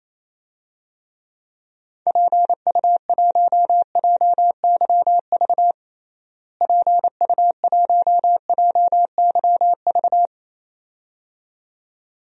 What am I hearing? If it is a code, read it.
Morse code, heard as a keyed tone, about 27 wpm, 700 Hz. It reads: PU1JYV PU1JYV
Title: DXCC-100 Call Signs - 6 Characters - Repeated 2x - 28wpm